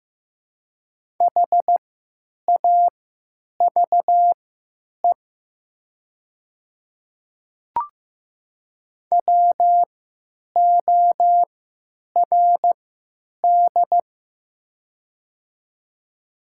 Morse code, heard as a keyed tone, about 15 wpm, spaced as for 9 wpm, 700 Hz. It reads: HAVE WORD
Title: 3x Character Spacing - Top 100 Words - 15wpm